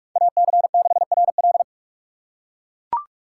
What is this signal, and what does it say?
Morse code, heard as a keyed tone, about 44 wpm, 700 Hz. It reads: AC6RL
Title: USA Call Signs - 6x Character Spacing - 45wpm